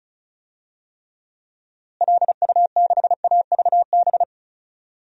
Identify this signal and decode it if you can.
Morse code, heard as a keyed tone, about 36 wpm, 700 Hz. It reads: LU6AVB
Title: DXCC-100 Call Signs - 6 Characters - Repeated 3x - 35wpm